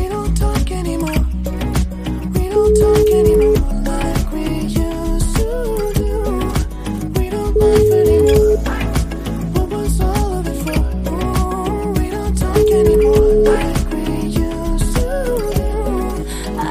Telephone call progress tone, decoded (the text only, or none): ringback tone